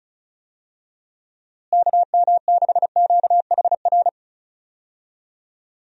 Morse code, heard as a keyed tone, about 34 wpm, 700 Hz. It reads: KM6QHR
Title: USA Call Signs - 6 Characters - Repeated 2x - 35wpm